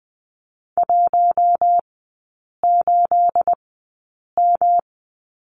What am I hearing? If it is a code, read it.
Morse code, heard as a keyed tone, about 19 wpm, spaced as for 8 wpm, 690 Hz. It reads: 18M
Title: 212 Lesson 34e - Number 8 - Three Characters 20wpm